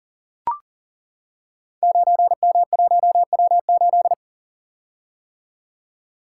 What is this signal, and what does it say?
Morse code, heard as a keyed tone, about 41 wpm, 690 Hz. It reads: 9M1W8
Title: Code Groups - 5 Letter:Numbers - 40wpm